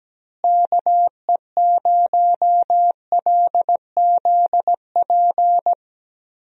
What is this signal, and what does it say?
Morse code, heard as a keyed tone, about 17 wpm, 700 Hz. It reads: KE0LZP
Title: USA Call Signs - 6 Characters - Repeated 3x - 17wpm